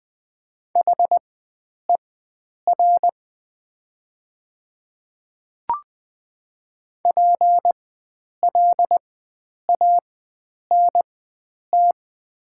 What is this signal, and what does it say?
Morse code, heard as a keyed tone, about 20 wpm, spaced as for 9 wpm, 700 Hz. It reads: HER PLANT
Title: 4x Character Spacing - Top 500 Words - 20wpm